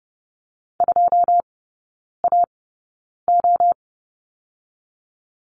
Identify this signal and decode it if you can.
Morse code, heard as a keyed tone, about 30 wpm, spaced as for 9 wpm, 700 Hz. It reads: 2AO
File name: 081 Lesson 14e - Number 2 - Three Characters 30wpm